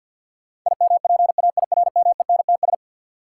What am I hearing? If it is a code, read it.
Morse code, heard as a keyed tone, about 53 wpm, 700 Hz. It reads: IMPAIRMENTS
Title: Sets of 5 Words Encore - New Word Drill - 50wpm